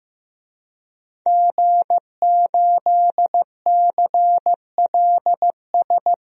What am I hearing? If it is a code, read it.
Morse code, heard as a keyed tone, about 15 wpm, 700 Hz. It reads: G8CLS
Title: DXCC-100 Call Signs - 5 Characters - Repeated 3x - 15wpm